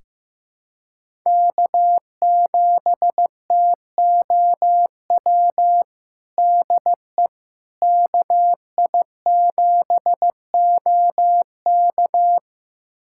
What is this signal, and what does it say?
Morse code, heard as a keyed tone, about 16 wpm, 700 Hz. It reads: K7TOW DE KI7OK
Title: USA Callsign DE Callsign - 5 Characters - 15wpm